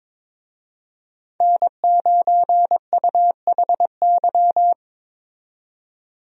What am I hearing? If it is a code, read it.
Morse code, heard as a keyed tone, about 22 wpm, 700 Hz. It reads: N9UHY
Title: USA Call Signs - 5 Characters - 22wpm